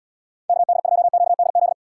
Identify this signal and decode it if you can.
Morse code, heard as a keyed tone, about 76 wpm, 700 Hz. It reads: KD2QDZ